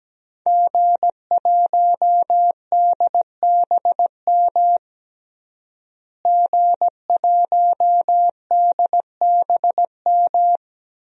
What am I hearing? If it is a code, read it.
Morse code, heard as a keyed tone, about 17 wpm, 700 Hz. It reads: G1DBM G1DBM
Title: DXCC-100 Call Signs - 5 Characters - Repeated 3x - 17wpm